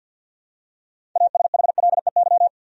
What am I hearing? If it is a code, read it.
Morse code, heard as a keyed tone, about 53 wpm, 700 Hz. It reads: ASHLEY
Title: USA Common Names - 50wpm